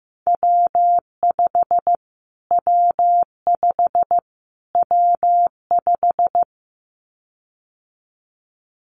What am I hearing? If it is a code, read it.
Morse code, heard as a keyed tone, about 15 wpm, 700 Hz. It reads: W5 W5 W5
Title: Callsign Prefixes for Top 25 DXCC - Repeated 3x - 15wpm